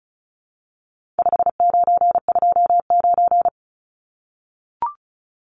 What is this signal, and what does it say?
Morse code, heard as a keyed tone, about 34 wpm, 700 Hz. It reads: 5929
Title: Numbers - 4-Digit 35wpm